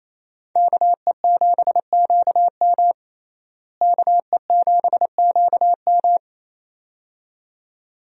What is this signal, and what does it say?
Morse code, heard as a keyed tone, about 28 wpm, 700 Hz. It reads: KE7QM KE7QM
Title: USA Call Signs - 5 Characters - Repeated 2x - 28wpm